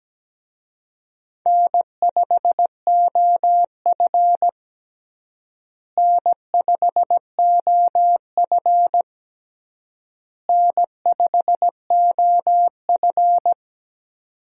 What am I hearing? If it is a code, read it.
Morse code, heard as a keyed tone, about 17 wpm, 700 Hz. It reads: N5OF N5OF N5OF
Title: USA Call Signs - 4 Characters - Repeated 3x - 17wpm